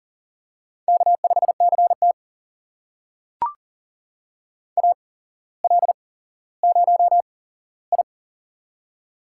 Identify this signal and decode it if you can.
Morse code, heard as a keyed tone, about 40 wpm, spaced as for 17 wpm, 700 Hz. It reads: K5CT AL0I
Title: USA Call Signs - 4 Characters - 8x Character Spacing - 40wpm